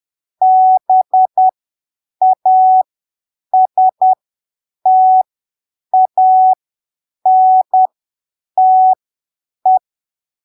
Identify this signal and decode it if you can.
Morse code, heard as a keyed tone, about 10 wpm, spaced as for 7 wpm, 750 Hz. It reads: BASTANTE
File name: arquivos / texto / texto 24 10ppm